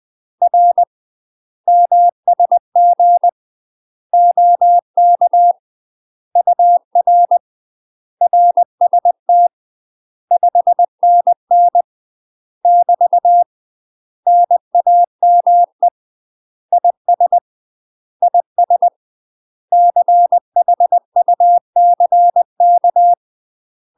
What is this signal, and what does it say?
Morse code, heard as a keyed tone, about 20 wpm, 700 Hz. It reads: R MSG OK UR RST 5NN <BT> NAME IS IS CHUCK
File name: UK MASTER QSOs TRAINING/20WPM